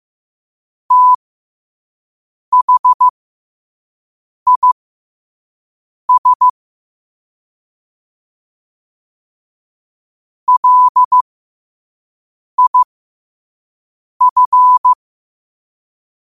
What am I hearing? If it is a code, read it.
Morse code, heard as a keyed tone, about 15 wpm, spaced as for 5 wpm, 1000 Hz. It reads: THIS LIF
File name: qotdfiles / 5wpm